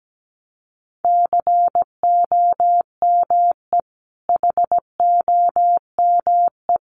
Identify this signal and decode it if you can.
Morse code, heard as a keyed tone, about 17 wpm, 700 Hz. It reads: COME HOME
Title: Sentences from Top 1000 Words 17wpm